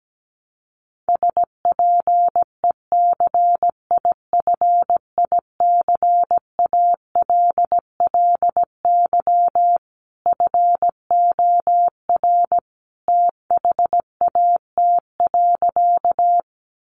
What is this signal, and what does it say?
Morse code, heard as a keyed tone, about 17 wpm, 700 Hz. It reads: SPECIFICALLY FOR THAT.
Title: Sentences from Top 1100 Words 17wpm